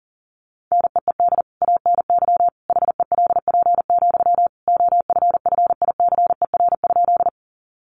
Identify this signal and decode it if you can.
Morse code, heard as a keyed tone, about 40 wpm, 700 Hz. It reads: NEED ANY HELP, OFFICER?
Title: Sentences from Top 1200 Words 40wpm